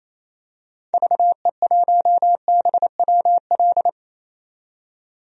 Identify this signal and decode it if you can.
Morse code, heard as a keyed tone, about 28 wpm, 700 Hz. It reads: VE1BWL
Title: DXCC-100 Call Signs - 6 Characters - 28wpm